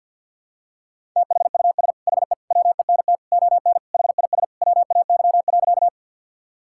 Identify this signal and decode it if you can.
Morse code, heard as a keyed tone, about 52 wpm, 690 Hz. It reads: THUS HE WENT ON HIS WAY.
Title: Sentences from Top 1000 Words - Review Part 5 - 50wpm